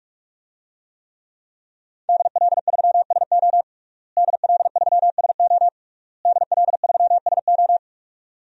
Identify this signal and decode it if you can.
Morse code, heard as a keyed tone, about 46 wpm, 700 Hz. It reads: DL3SO DL3SO DL3SO
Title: DXCC-100 Call Signs - 5 Characters - Repeated 3x - 45wpm